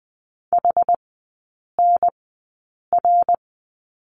Morse code, heard as a keyed tone, about 20 wpm, spaced as for 8 wpm, 700 Hz. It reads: HNR
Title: 114 Lesson 19e - Letter w - Three Characters 20wpm